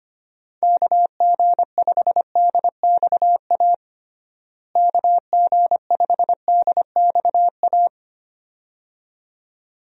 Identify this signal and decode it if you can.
Morse code, heard as a keyed tone, about 25 wpm, 700 Hz. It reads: KG5DXA KG5DXA
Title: USA Call Signs - 6 Characters - Repeated 2x - 25wpm